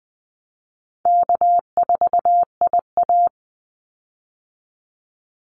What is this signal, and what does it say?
Morse code, heard as a keyed tone, about 20 wpm, 700 Hz. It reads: K4IA
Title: Easy Call Signs Repeated 2x - 20wpm